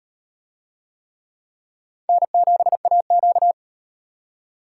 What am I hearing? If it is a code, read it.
Morse code, heard as a keyed tone, about 36 wpm, 700 Hz. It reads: N7AQ